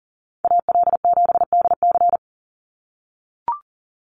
Morse code, heard as a keyed tone, about 40 wpm, 700 Hz. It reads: AL7DC
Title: USA Call Signs 40wpm